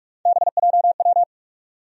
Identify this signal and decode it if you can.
Morse code, heard as a keyed tone, about 46 wpm, 700 Hz. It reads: BJW
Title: USA Call Sign Postfixes - 3 Letters 45wpm